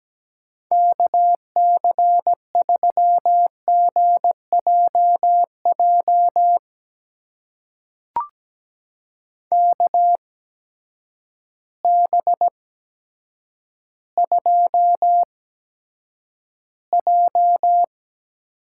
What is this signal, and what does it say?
Morse code, heard as a keyed tone, about 17 wpm, spaced as for 8 wpm, 700 Hz. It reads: KC3GJJ KB2J
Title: USA Call Signs - 6 Characters - 8x Character Spacing - 17wpm